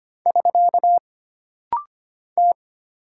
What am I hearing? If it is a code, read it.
Morse code, heard as a keyed tone, about 25 wpm, 700 Hz. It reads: <SK> T
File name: Single Character (Letter:Number:Punctuation:Prosign) - Rapid-Fire - 25wpm